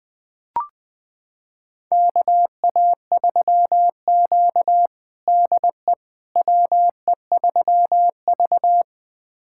Morse code, heard as a keyed tone, about 20 wpm, 700 Hz. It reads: KA3Q DE WE3V
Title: USA Callsign DE Callsign - 4 Characters - 20wpm